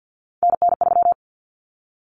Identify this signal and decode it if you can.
Morse code, heard as a keyed tone, about 50 wpm, 700 Hz. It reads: NN3
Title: Common Cut Numbers - 3-Digit 50wpm